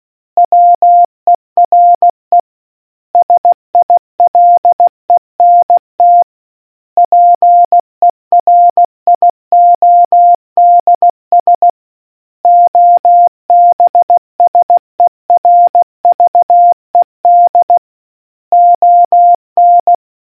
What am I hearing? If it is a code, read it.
Morse code, heard as a keyed tone, about 16 wpm, 700 Hz. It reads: WERE SILENT PERIODS OBSERVED ON